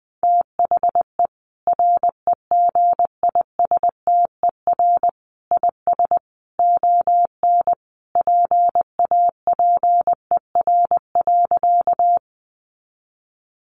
Morse code, heard as a keyed tone, about 19 wpm, 700 Hz. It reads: THE REGISTER IS ON PAPER.
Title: Sentences from Top 1200 Words 20wpm